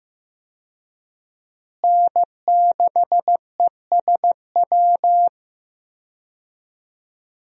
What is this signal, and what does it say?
Morse code, heard as a keyed tone, about 16 wpm, 700 Hz. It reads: N6ESW
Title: USA Call Signs - 5 Characters - Repeated 2x - 15wpm